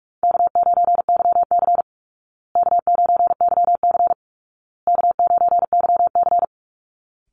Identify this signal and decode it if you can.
Morse code, heard as a keyed tone, about 47 wpm, 700 Hz. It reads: K9YC K9YC K9YC
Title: Easy Call Signs Repeated 3x - 45wpm